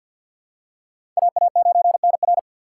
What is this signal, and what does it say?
Morse code, heard as a keyed tone, about 47 wpm, 690 Hz. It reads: AA9NR
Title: USA Call Signs - 5 Characters - 6x Character Spacing - 50wpm